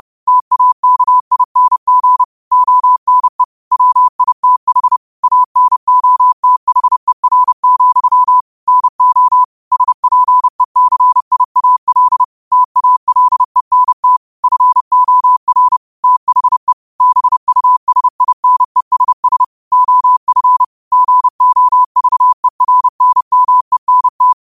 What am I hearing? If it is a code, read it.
Morse code, heard as a keyed tone, about 30 wpm, 1000 Hz. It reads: TAKING ONE WITH ANOTHER, NO SPECIAL TALENT FOR THE BUSINESS OF GOVERNMENT
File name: qotdfiles / 30wpm